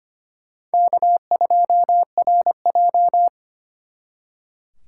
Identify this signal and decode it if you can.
Morse code, heard as a keyed tone, about 29 wpm, 700 Hz. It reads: K2RJ
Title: Speed Racing - USA 1x2 Call Signs 25wpm